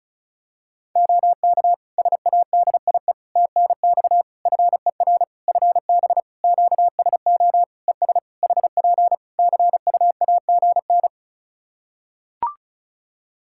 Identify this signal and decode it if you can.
Morse code, heard as a keyed tone, about 34 wpm, 700 Hz. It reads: OK SADIE TNX FER FB QSO ES HP CUAGN
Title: Practice QSO - Exchange 4 - Ending - 35wpm